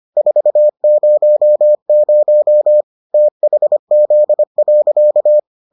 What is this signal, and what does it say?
Morse code, heard as a keyed tone, about 25 wpm, 600 Hz. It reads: 400 THZ.